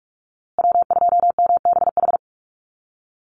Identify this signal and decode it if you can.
Morse code, heard as a keyed tone, about 44 wpm, 690 Hz. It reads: W2MBH
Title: USA Call Signs Repeated 2x 45wpm